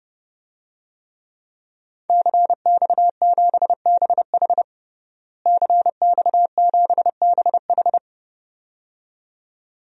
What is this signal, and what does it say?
Morse code, heard as a keyed tone, about 32 wpm, 700 Hz. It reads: CX7BH CX7BH
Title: DXCC-100 Call Signs - 5 Characters - Repeated 2x - 30wpm